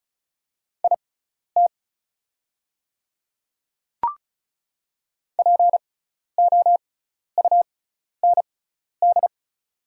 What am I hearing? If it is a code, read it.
Morse code, heard as a keyed tone, about 36 wpm, spaced as for 12 wpm, 700 Hz. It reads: IT POUND
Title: 6x Character Spacing - Top 500 Words - 35wpm